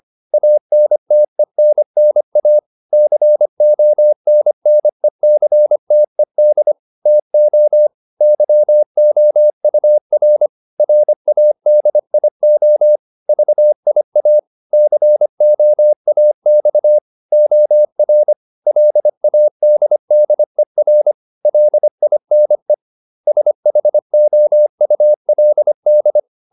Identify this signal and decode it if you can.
Morse code, heard as a keyed tone, about 25 wpm, 600 Hz. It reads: ANTENNA CONNECTED TO YOUR RADIO VIA COAX OR LADDER LINE SHOULD